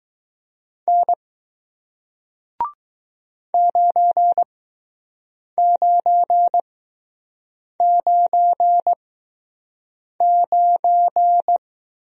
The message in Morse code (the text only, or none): N 9999